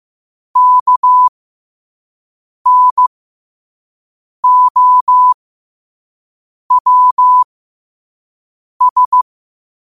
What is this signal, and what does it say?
Morse code, heard as a keyed tone, about 15 wpm, spaced as for 5 wpm, 1000 Hz. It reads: KNOWS